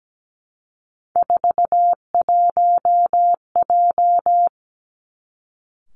Morse code, heard as a keyed tone, about 17 wpm, 700 Hz. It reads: 41J
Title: Three Challenging Letter-Number Combinations 17wpm